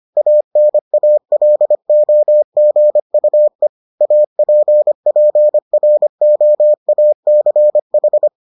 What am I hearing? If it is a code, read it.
Morse code, heard as a keyed tone, about 25 wpm, 600 Hz. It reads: ANALOGUE APPROACH